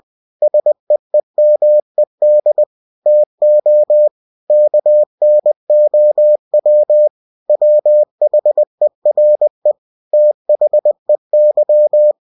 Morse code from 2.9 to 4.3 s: TO